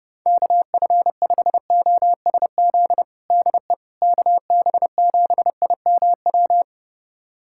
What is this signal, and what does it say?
Morse code, heard as a keyed tone, about 32 wpm, 700 Hz. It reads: KF5OSZ DE KB7IMW
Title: USA Callsign DE Callsign 30wpm